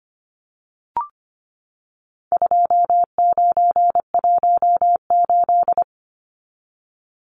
Morse code, heard as a keyed tone, about 25 wpm, 700 Hz. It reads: 2918
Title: Numbers - 4-Digit 25wpm